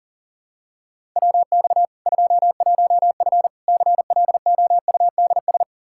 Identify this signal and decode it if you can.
Morse code, heard as a keyed tone, about 40 wpm, 700 Hz. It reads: WX 21F CLOUDS
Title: K1USN SST QSO Practice 40wpm